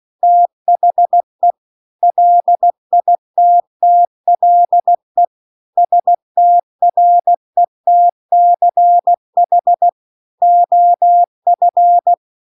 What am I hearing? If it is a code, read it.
Morse code, heard as a keyed tone, about 16 wpm, 700 Hz. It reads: THE LITTLE STRETCH OF